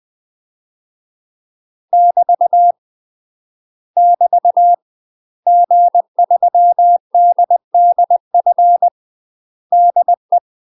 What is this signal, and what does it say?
Morse code, heard as a keyed tone, about 20 wpm, 700 Hz. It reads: <BT> <BT> G3DDF DE
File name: UK MASTER QSOs TRAINING / 20WPM / UK MASTER QSOs 0049